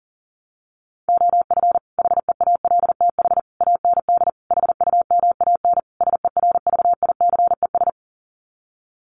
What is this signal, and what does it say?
Morse code, heard as a keyed tone, about 40 wpm, 700 Hz. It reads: OF HEALTH AND HUMAN SERVICES
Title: Sets of 5 Words 40wpm